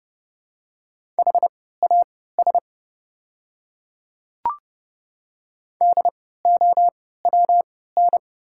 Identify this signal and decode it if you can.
Morse code, heard as a keyed tone, about 30 wpm, spaced as for 17 wpm, 700 Hz. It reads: HAS DOWN
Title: 3x Character Spacing - Top 100 Words - 30wpm